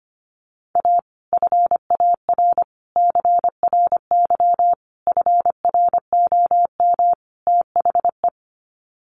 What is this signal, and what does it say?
Morse code, heard as a keyed tone, about 25 wpm, 700 Hz. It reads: A FAR CRY FROM THE
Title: Sets of 5 Words Encore 25wpm